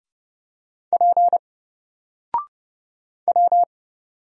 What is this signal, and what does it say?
Morse code, heard as a keyed tone, about 30 wpm, 700 Hz. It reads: P W
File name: Single-Letter-Number-30wpm